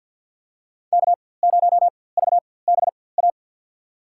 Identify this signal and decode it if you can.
Morse code, heard as a keyed tone, about 53 wpm, spaced as for 24 wpm, 700 Hz. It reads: K0VBA
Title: USA Call Signs - 5 Characters - 4x Character Spacing - 50wpm